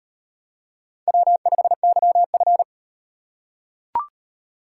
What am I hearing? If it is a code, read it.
Morse code, heard as a keyed tone, about 35 wpm, 700 Hz. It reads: W5YF